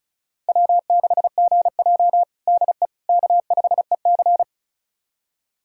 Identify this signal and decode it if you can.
Morse code, heard as a keyed tone, about 36 wpm, 700 Hz. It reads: W6GJ DE K5EC
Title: USA Callsign DE Callsign - 4 Characters - 35wpm